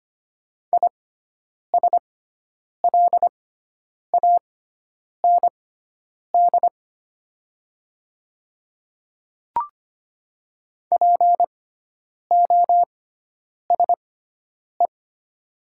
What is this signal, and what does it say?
Morse code, heard as a keyed tone, about 25 wpm, spaced as for 9 wpm, 700 Hz. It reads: ISLAND POSE
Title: 6x Character Spacing - Top 500 Words - 25wpm